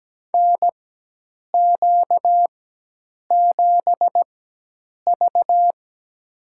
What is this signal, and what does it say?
Morse code, heard as a keyed tone, about 17 wpm, spaced as for 8 wpm, 700 Hz. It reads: NQ7V
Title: USA Call Signs - 4 Characters - 4x Character Spacing - 17wpm